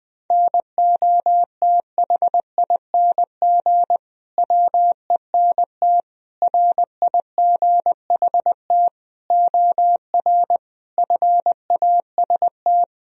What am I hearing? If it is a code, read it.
Morse code, heard as a keyed tone, about 20 wpm, 700 Hz. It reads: NOTHING WENT RIGHT OR FAST
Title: Sentences from Top 1000 Words - Review Part 2 - 20wpm